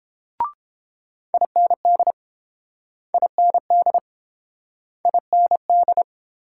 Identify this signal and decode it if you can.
Morse code, heard as a keyed tone, about 30 wpm, 700 Hz. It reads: IND IND IND